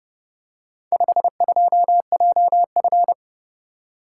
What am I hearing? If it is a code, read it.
Morse code, heard as a keyed tone, about 29 wpm, 700 Hz. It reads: 52JF